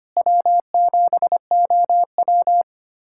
Morse code, heard as a keyed tone, about 25 wpm, 700 Hz. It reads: W7OW
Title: USA Callsign DE Callsign - 4 Characters - 25wpm